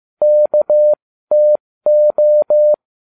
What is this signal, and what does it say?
Morse code, heard as a keyed tone, about 15 wpm, spaced as for 13 wpm, 600 Hz. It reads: KTO